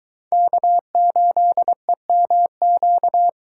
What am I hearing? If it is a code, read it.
Morse code, heard as a keyed tone, about 23 wpm, 700 Hz. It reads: K8EMQ